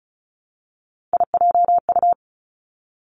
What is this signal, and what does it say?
Morse code, heard as a keyed tone, about 36 wpm, 700 Hz. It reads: IJU
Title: Three Challenging Letter-Number Combinations 35wpm